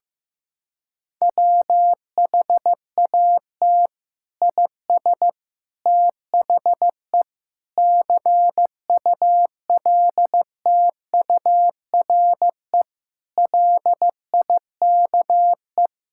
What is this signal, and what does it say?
Morse code, heard as a keyed tone, about 15 wpm, 700 Hz. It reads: WHAT IS THE CULTURE LIKE
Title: Sentences from Top 1000 Words - Review Part 6 - 15wpm